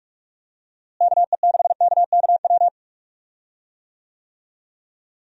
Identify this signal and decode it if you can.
Morse code, heard as a keyed tone, about 46 wpm, 700 Hz. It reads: KE6KKW
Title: USA Call Signs - 6 Characters - Repeated 3x - 45wpm